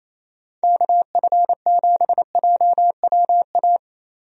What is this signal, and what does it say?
Morse code, heard as a keyed tone, about 28 wpm, 700 Hz. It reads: KF7JWA
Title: USA Call Signs - 6 Characters - Repeated 2x - 28wpm